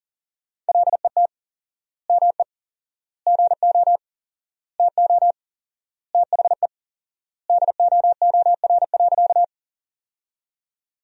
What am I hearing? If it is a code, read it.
Morse code, heard as a keyed tone, about 40 wpm, 700 Hz. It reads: LET ME GO TO THE DOOR.
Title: 4x Word-Spacing - Sentences from Top 300 Words - 40wpm